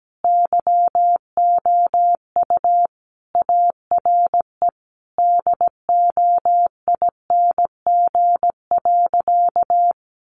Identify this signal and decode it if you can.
Morse code, heard as a keyed tone, about 17 wpm, 700 Hz. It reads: YOU ARE DOING.
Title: Sentences from Top 1400 Words 17wpm